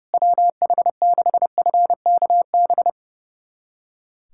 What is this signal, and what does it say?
Morse code, heard as a keyed tone, about 30 wpm, 700 Hz. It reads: WH6FKB
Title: USA Callsign DE Callsign - 6 Characters - 30wpm